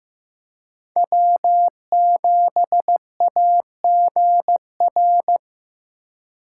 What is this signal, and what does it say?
Morse code, heard as a keyed tone, about 15 wpm, 700 Hz. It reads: W7AGR
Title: USA Call Signs - 5 Characters - 2x Character Spacing - 15wpm